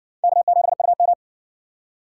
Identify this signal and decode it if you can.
Morse code, heard as a keyed tone, about 58 wpm, 700 Hz. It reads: K7UM